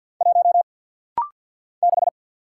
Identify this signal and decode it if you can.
Morse code, heard as a keyed tone, about 48 wpm, 700 Hz. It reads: 1 6
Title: Single Letter-Number - Rapid-Fire 50wpm